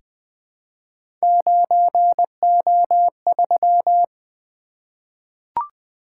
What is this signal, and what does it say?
Morse code, heard as a keyed tone, about 20 wpm, 700 Hz. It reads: 9O3
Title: Three Letter-Number Combinations - 20wpm